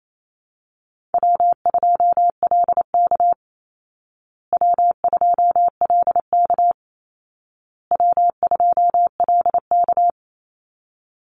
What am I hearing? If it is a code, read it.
Morse code, heard as a keyed tone, about 28 wpm, 700 Hz. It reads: W2LK W2LK W2LK
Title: Call Signs Repeated 2x - 28wpm